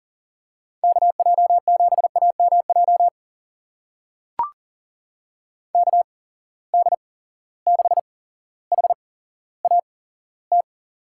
Morse code, heard as a keyed tone, about 40 wpm, spaced as for 17 wpm, 700 Hz. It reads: KJ7AMJ KD6HAT
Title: USA Call Signs - 6 Characters - 8x Character Spacing - 40wpm